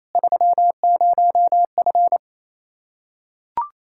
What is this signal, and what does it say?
Morse code, heard as a keyed tone, about 28 wpm, 700 Hz. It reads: 30F
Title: Practice QSO - Exchange 2 - Info - 28wpm